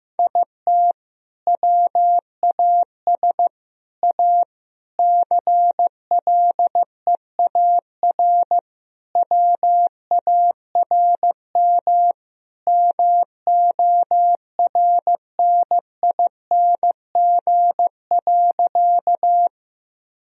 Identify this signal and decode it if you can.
Morse code, heard as a keyed tone, about 16 wpm, 700 Hz. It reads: IT WAS A CLEAR WARM MORNING.
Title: Sentences from Top 1000 Words - Review Part 2 - 15wpm